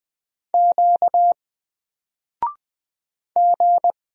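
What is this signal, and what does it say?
Morse code, heard as a keyed tone, about 20 wpm, 700 Hz. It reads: Q G
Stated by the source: Single letter and number Morse drill at 20 wpm, from N4LCD.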